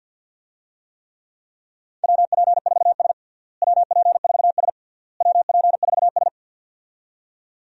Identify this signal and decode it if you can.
Morse code, heard as a keyed tone, about 52 wpm, 700 Hz. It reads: WP4S WP4S WP4S
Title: USA Call Signs - 4 Characters - Repeated 3x - 50wpm